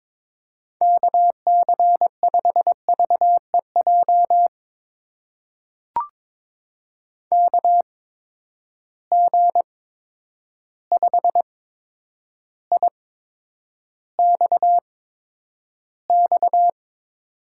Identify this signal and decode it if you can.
Morse code, heard as a keyed tone, about 22 wpm, spaced as for 9 wpm, 700 Hz. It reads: KC5VEJ KG5IXX
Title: USA Call Signs - 6 Characters - 8x Character Spacing - 22wpm